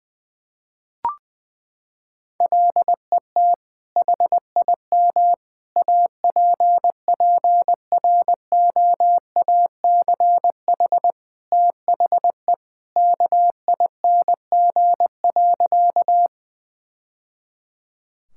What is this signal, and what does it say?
Morse code, heard as a keyed tone, about 20 wpm, 700 Hz. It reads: LET HIM APPROACH THE KING.
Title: Sentences from Top 1000 Words - Review Part 6 - 20wpm